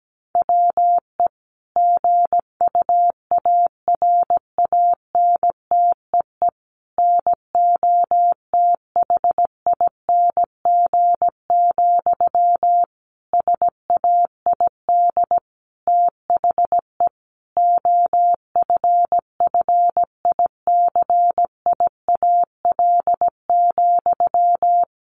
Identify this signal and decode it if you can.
Morse code, heard as a keyed tone, about 17 wpm, 700 Hz. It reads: WE GUARANTEE NOTHING, SAID THE OFFICIAL,
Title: A Sound Of Thunder 17wpm